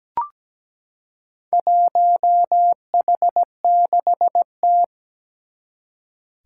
Morse code, heard as a keyed tone, about 17 wpm, 700 Hz. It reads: 1H6T
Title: Four Challenging Letter-Number Combinations 17wpm